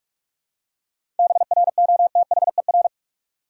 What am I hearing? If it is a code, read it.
Morse code, heard as a keyed tone, about 46 wpm, 700 Hz. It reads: BROTHER
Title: Seven Letter Words - 45wpm